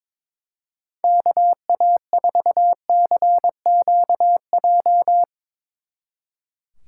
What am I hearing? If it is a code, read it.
Morse code, heard as a keyed tone, about 24 wpm, 700 Hz. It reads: KA4CQJ
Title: Speed Racing - USA 2x3 Call Signs 22wpm